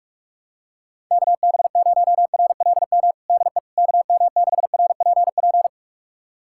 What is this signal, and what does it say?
Morse code, heard as a keyed tone, about 46 wpm, 700 Hz. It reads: KB0RLM DE KM6RPP
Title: USA Callsign DE Callsign - 6 Characters - 45wpm